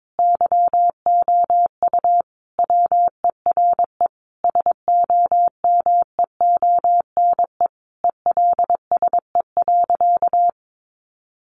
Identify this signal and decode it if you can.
Morse code, heard as a keyed tone, about 22 wpm, 700 Hz. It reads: YOU WERE SOMEONE ELSE.
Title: Sentences from Top 800 Words 22wpm